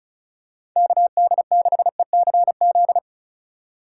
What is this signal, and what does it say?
Morse code, heard as a keyed tone, about 34 wpm, 700 Hz. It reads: KD6ECZ
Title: USA Call Signs - 6 Characters - 3x Character Spacing - 35wpm